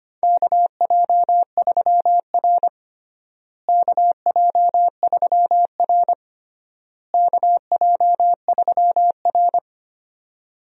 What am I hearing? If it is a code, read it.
Morse code, heard as a keyed tone, about 25 wpm, 700 Hz. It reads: KJ3R KJ3R KJ3R
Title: USA Call Signs - 4 Characters - Repeated 3x - 25wpm